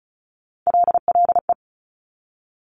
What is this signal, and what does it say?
Morse code, heard as a keyed tone, about 36 wpm, 700 Hz. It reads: LLE
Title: Top Three Letter Combinations in Words 35wpm